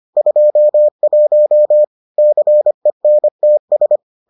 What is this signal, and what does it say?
Morse code, heard as a keyed tone, about 25 wpm, 600 Hz. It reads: 21 CENTS